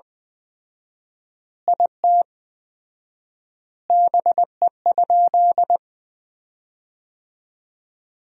Morse code, heard as a keyed tone, about 20 wpm, 700 Hz. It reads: IT BE?
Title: 4x Word-Spacing - Sentences from Top 300 Words - 20wpm